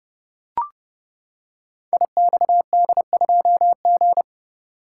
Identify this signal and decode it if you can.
Morse code, heard as a keyed tone, about 30 wpm, 700 Hz. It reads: IXD2G
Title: Code Groups - 5 Letter:Numbers - 30wpm